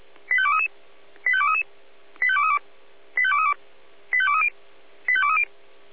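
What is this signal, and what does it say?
Five-tone selective call (ZVEI), heard as quick runs of five tones, tones of about 70 ms. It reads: 85233 85233 85232 85232 85230 85230
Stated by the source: Five-tone ZVEI selective call 85885, received by SDR.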